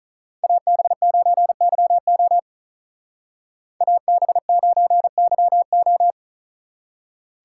Morse code, heard as a keyed tone, about 42 wpm, 700 Hz. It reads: AB9YO AB9YO
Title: Speed Racing - USA Call Signs 35wpm